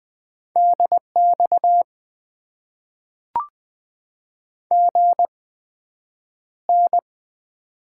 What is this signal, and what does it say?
Morse code, heard as a keyed tone, about 20 wpm, spaced as for 9 wpm, 700 Hz. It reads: DX GN
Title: Top 90 QSO Elements - 8x Character Spacing - 20wpm